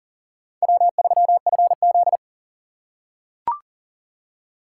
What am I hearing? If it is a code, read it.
Morse code, heard as a keyed tone, about 40 wpm, 700 Hz. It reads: W3FZ
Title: USA Call Signs - 4 Characters - 2x Character Spacing - 40wpm